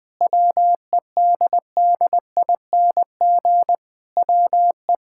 Morse code, heard as a keyed tone, about 20 wpm, 700 Hz. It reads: WEDDING WE
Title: Sentences with 7-Word Phrases - 20wpm